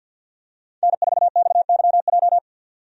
Morse code, heard as a keyed tone, about 52 wpm, 700 Hz. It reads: N4XYJ
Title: USA Call Signs - 5 Characters - 50wpm